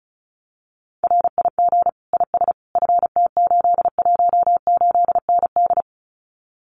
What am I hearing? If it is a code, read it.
Morse code, heard as a keyed tone, about 36 wpm, 700 Hz. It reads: RIG IS FT818ND
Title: Common QSO Phrases Encore 35wpm